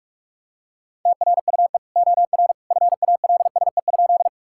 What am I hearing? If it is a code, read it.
Morse code, heard as a keyed tone, about 46 wpm, 700 Hz. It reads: TRUE OR FALSE?
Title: Sentences from Top 1600 Words 45wpm